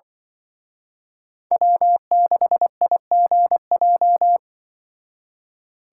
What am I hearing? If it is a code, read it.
Morse code, heard as a keyed tone, about 24 wpm, 700 Hz. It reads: W6IGJ
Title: Speed Racing - USA 1x3 Call Signs 20wpm